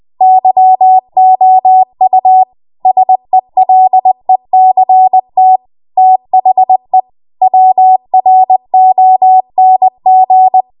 Morse code, heard as a keyed tone, about 20 wpm, 750 Hz. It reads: YOU SELECT THE WRONG
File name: Morse / 20WPM